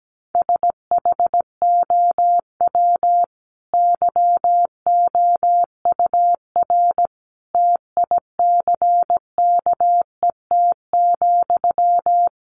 Morse code, heard as a keyed tone, about 17 wpm, 700 Hz. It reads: SHOW YOUR TICKET,
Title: Sentences from Top 1400 Words 17wpm